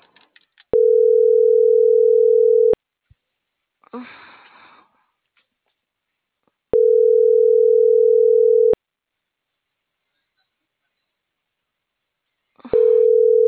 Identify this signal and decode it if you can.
Ringback tone (North America), 440 + 480 Hz.